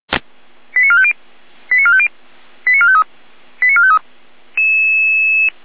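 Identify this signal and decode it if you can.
Five-tone selective call (ZVEI), heard as quick runs of five tones, tones of about 70 ms. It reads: 89450 89450 89453 89453
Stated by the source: ZVEI five-tone selective call received on an SDR.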